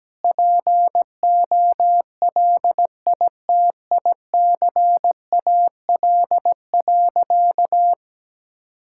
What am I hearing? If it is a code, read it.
Morse code, heard as a keyed tone, about 17 wpm, 700 Hz. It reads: POLITICAL.
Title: Sentences from Top 1000 Words - Review Part 5 - 17wpm